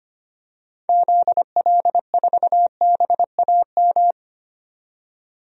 Morse code, heard as a keyed tone, about 25 wpm, 700 Hz. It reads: ZL4BAM
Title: DXCC-100 Call Signs - 6 Characters - 25wpm